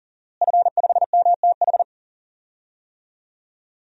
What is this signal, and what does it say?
Morse code, heard as a keyed tone, about 37 wpm, 700 Hz. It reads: F5MTH
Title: DXCC-100 Call Signs - 5 Characters - Repeated 2x - 40wpm